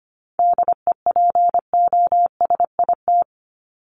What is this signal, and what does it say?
Morse code, heard as a keyed tone, about 25 wpm, 700 Hz. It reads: DEPOSIT
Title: Top 1251-1500 Words 25wpm